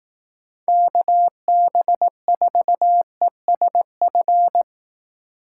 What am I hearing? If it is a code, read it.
Morse code, heard as a keyed tone, about 18 wpm, 700 Hz. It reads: KB4ESF
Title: Speed Racing - USA 2x3 Call Signs 15wpm